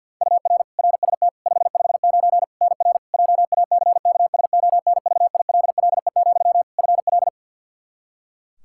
Morse code, heard as a keyed tone, about 51 wpm, 700 Hz. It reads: UR RST 559 NR JACKSONVILLE, FL
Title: Practice QSO - Exchange 1 - Beginning - 50wpm